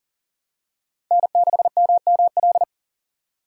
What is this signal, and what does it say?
Morse code, heard as a keyed tone, about 40 wpm, 700 Hz. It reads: N6MML
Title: USA Call Signs - 5 Characters - 6x Character Spacing - 40wpm